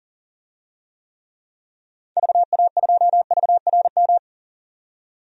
Morse code, heard as a keyed tone, about 40 wpm, 700 Hz. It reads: VA2VRM
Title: DXCC-100 Call Signs - 6 Characters - 40wpm